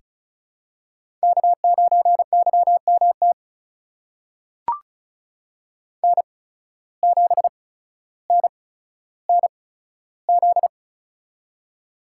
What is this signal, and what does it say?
Morse code, heard as a keyed tone, about 36 wpm, spaced as for 15 wpm, 700 Hz. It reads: K9YMT N7NNZ
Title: USA Call Signs - 5 Characters - 8x Character Spacing - 35wpm